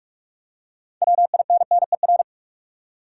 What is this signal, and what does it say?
Morse code, heard as a keyed tone, about 46 wpm, 700 Hz. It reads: WINNER
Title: Six Letter Words - 45wpm